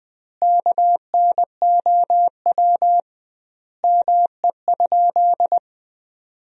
Morse code, heard as a keyed tone, about 21 wpm, 700 Hz. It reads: KNOW ME?
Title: 2x Word-Spacing - Sentences from Top 200 Words - 20wpm